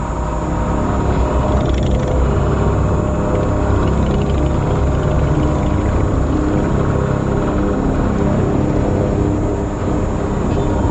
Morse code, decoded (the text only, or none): none